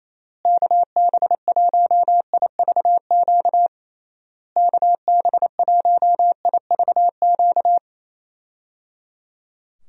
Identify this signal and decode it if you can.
Morse code, heard as a keyed tone, about 28 wpm, 700 Hz. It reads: KB1IVQ KB1IVQ
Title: USA Call Signs - 6 Characters - Repeated 3x - 28wpm